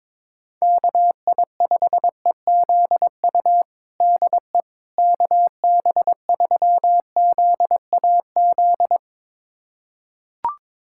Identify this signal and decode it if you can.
Morse code, heard as a keyed tone, about 22 wpm, 700 Hz. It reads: KI5EZU DE KB3ZAZ
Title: USA Callsign DE Callsign - 6 Characters - 22wpm